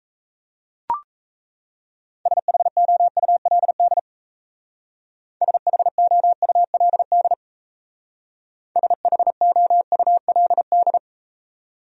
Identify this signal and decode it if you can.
Morse code, heard as a keyed tone, about 39 wpm, 700 Hz. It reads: SHOULD SHOULD SHOULD